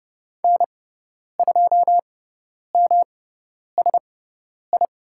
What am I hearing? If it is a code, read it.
Morse code, heard as a keyed tone, about 30 wpm, spaced as for 10 wpm, 700 Hz. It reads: N2MSI
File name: Lesson 18g - Letter m - Call Signs 30wpm